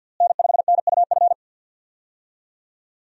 Morse code, heard as a keyed tone, about 52 wpm, 700 Hz. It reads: N5NUF
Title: USA Call Signs - 5 Characters - Repeated 3x - 50wpm